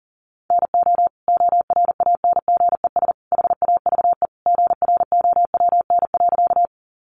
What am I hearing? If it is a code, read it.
Morse code, heard as a keyed tone, about 40 wpm, 700 Hz. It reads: NO ORANGES HAVE GROWN.
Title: Easy Phrases 40wpm